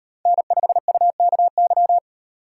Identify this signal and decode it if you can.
Morse code, heard as a keyed tone, about 38 wpm, 700 Hz. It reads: N5UKY